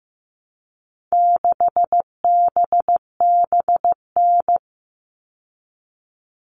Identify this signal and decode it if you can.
Morse code, heard as a keyed tone, about 15 wpm, 700 Hz. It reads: 6BBN
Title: All Cut Numbers - 4-Digit 15wpm